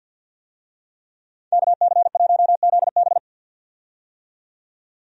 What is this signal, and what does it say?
Morse code, heard as a keyed tone, about 51 wpm, 700 Hz. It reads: KC1ZB
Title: USA Call Signs - 5 Characters - 50wpm